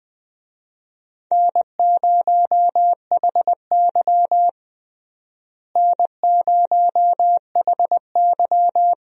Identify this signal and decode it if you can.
Morse code, heard as a keyed tone, about 20 wpm, 700 Hz. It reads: N0HY N0HY
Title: USA Call Signs - 4 Characters - Repeated 3x - 20wpm